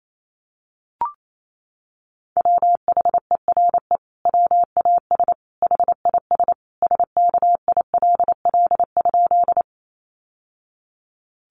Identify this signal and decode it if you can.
Morse code, heard as a keyed tone, about 28 wpm, 700 Hz. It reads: WHERE WAS HIS SKILL?
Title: Sentences from Top 1200 Words 28wpm